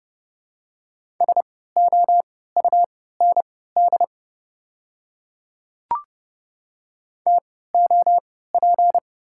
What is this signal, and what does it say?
Morse code, heard as a keyed tone, about 32 wpm, spaced as for 17 wpm, 700 Hz. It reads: SOUND TOP
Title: 3x Character Spacing - Top 500 Words - 30wpm